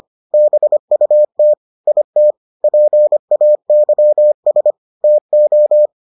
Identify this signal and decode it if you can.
Morse code, heard as a keyed tone, about 25 wpm, 600 Hz. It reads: BUT IT PAYS TO